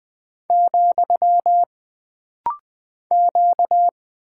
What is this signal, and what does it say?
Morse code, heard as a keyed tone, about 20 wpm, 700 Hz. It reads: , Q
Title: Single Character (Letter:Number:Punctuation:Prosign) - Rapid-Fire - 20wpm